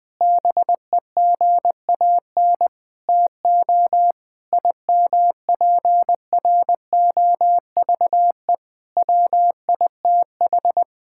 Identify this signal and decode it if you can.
Morse code, heard as a keyed tone, about 20 wpm, 700 Hz. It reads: BEGAN TO IMPROVE WITH
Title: Sentences from Top 1000 Words - Review Part 5 - 20wpm